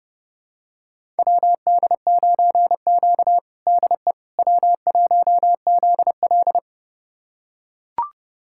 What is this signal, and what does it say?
Morse code, heard as a keyed tone, about 32 wpm, 700 Hz. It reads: WD9Q DE W1ZL
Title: USA Callsign DE Callsign - 4 Characters - 30wpm